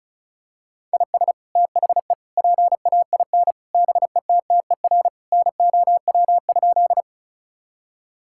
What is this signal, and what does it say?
Morse code, heard as a keyed tone, about 36 wpm, 700 Hz. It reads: IS THE PAIN BETTER NOW?
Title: Sentences from Top 1600 Words 35wpm